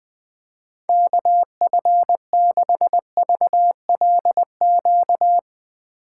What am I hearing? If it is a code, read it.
Morse code, heard as a keyed tone, about 20 wpm, 700 Hz. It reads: KF6VLQ
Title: USA Call Signs - 6 Characters - 8x Character Spacing - 20wpm